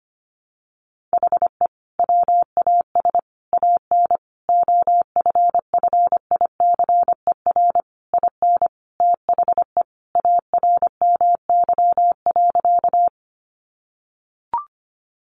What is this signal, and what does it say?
Morse code, heard as a keyed tone, about 25 wpm, 700 Hz. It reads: HE WAS AN OFFICER IN THE ARMY.
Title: Sentences from Top 1200 Words 25wpm